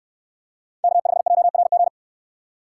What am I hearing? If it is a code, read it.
Morse code, heard as a keyed tone, about 69 wpm, 700 Hz. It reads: K5JRG